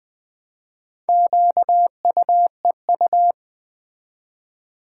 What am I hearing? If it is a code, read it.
Morse code, heard as a keyed tone, about 20 wpm, 700 Hz. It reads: QUEU
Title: Build A Word - 1 Letter At A Time - 5-Letter Words - 20wpm